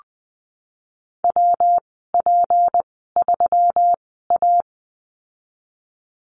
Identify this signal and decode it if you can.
Morse code, heard as a keyed tone, about 20 wpm, spaced as for 15 wpm, 700 Hz. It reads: WP3A
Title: 219 Lesson 35g - Number 0 - Call Signs 20wpm